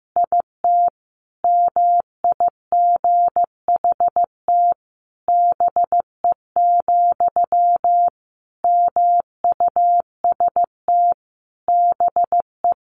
Morse code, heard as a keyed tone, about 15 wpm, 700 Hz. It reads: IT MIGHT BE, MUST BE
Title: Sentences from Top 1100 Words 15wpm